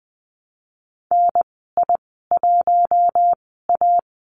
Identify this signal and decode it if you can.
Morse code, heard as a keyed tone, about 20 wpm, spaced as for 14 wpm, 700 Hz. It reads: NI1A
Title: 055 Lesson 10g - Letter r - Call Signs 20wpm